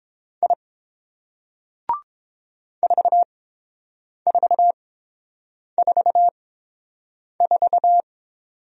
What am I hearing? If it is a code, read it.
Morse code, heard as a keyed tone, about 29 wpm, spaced as for 7 wpm, 700 Hz. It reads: I 4444